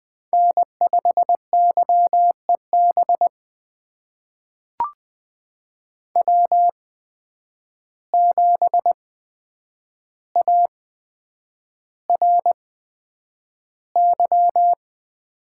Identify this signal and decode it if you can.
Morse code, heard as a keyed tone, about 20 wpm, spaced as for 9 wpm, 700 Hz. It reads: N5YEB W7ARY
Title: USA Call Signs - 5 Characters - 8x Character Spacing - 20wpm